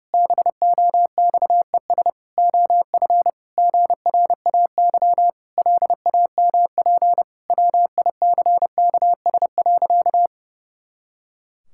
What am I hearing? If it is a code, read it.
Morse code, heard as a keyed tone, about 30 wpm, 700 Hz. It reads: BOXES OF GRAY LAMP WICKS.
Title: Pangram Challenge v2 - 30wpm